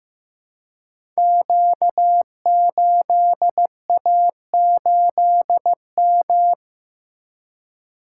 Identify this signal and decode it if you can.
Morse code, heard as a keyed tone, about 15 wpm, 700 Hz. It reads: Q8A8M
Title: Code Groups - 5 Letter:Numbers - 15wpm